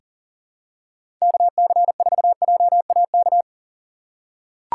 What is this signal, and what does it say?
Morse code, heard as a keyed tone, about 40 wpm, 700 Hz. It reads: KC4JAK
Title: USA Call Signs - 6 Characters - 2x Character Spacing - 40wpm